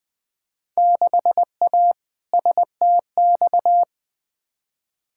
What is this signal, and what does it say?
Morse code, heard as a keyed tone, about 20 wpm, 700 Hz. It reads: 6A STX
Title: Field Day Exchanges - 20wpm